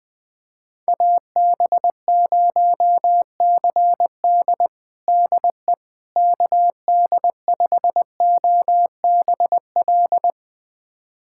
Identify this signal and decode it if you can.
Morse code, heard as a keyed tone, about 20 wpm, 700 Hz. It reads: AB0CD DE KD5OBL
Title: USA Callsign DE Callsign 20wpm